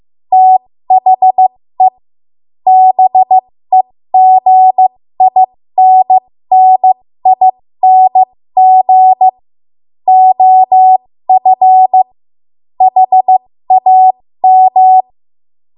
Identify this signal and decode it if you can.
Morse code, heard as a keyed tone, about 15 wpm, spaced as for 13 wpm, 750 Hz. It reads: THE BEGINNING OF HAM
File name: Morse / 13WPM